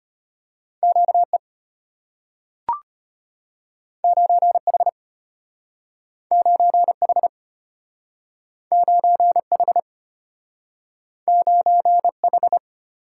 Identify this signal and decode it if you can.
Morse code, heard as a keyed tone, about 34 wpm, 700 Hz. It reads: QE 9H 9H 9H 9H